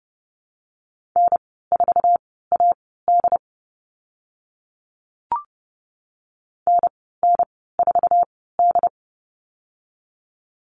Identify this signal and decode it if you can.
Morse code, heard as a keyed tone, about 30 wpm, spaced as for 17 wpm, 700 Hz. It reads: N4AD NN4D
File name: Lesson 14g - Number 2 - Call Signs 30wpm